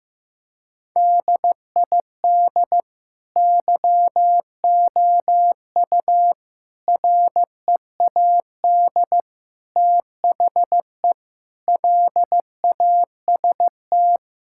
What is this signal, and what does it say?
Morse code, heard as a keyed tone, about 15 wpm, 700 Hz. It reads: DID YOU READ THE LAST
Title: Sentences from Top 1000 Words - Review Part 5 - 15wpm